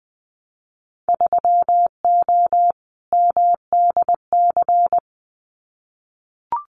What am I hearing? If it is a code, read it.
Morse code, heard as a keyed tone, about 20 wpm, 700 Hz. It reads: 3O MDC
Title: Winter Field Day Exchanges Encore 20wpm